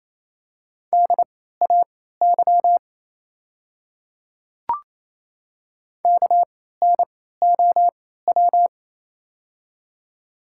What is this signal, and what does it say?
Morse code, heard as a keyed tone, about 28 wpm, spaced as for 16 wpm, 700 Hz. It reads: DAY KNOW
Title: 3x Character Spacing - Top 100 Words - 28wpm